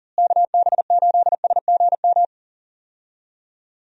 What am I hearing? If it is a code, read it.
Morse code, heard as a keyed tone, about 40 wpm, 700 Hz. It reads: KB8SGM
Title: USA Call Signs - 6 Characters - Repeated 3x - 40wpm